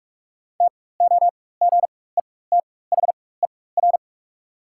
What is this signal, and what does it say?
Morse code, heard as a keyed tone, about 45 wpm, spaced as for 21 wpm, 690 Hz. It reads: TOGETHER
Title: 4x Character Spacing - Top 300 Words - 45wpm